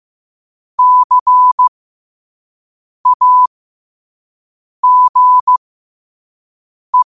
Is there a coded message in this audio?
Yes: Morse code from 0.8 to 7.0 s, heard as a keyed tone, about 15 wpm, spaced as for 5 wpm, 1000 Hz. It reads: CAGE